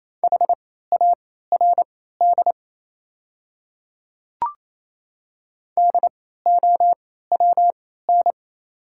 Morse code, heard as a keyed tone, about 28 wpm, spaced as for 16 wpm, 690 Hz. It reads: HARD DOWN
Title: 3x Character Spacing - Top 300 Words - 28wpm